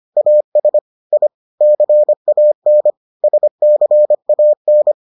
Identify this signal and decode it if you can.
Morse code, heard as a keyed tone, about 25 wpm, 600 Hz. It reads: AS I CAN SCAN